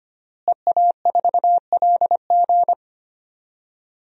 Morse code, heard as a keyed tone, about 25 wpm, 700 Hz. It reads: EA4LG
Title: DXCC-100 Call Signs - 5 Characters - Repeated 3x - 25wpm